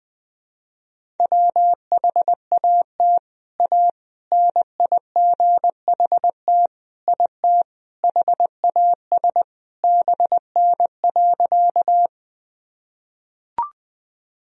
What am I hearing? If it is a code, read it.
Morse code, heard as a keyed tone, about 20 wpm, 700 Hz. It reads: WHAT A NIGHT IT HAS BN.
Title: Sentences from Top 300 Words and Abbreviations 20wpm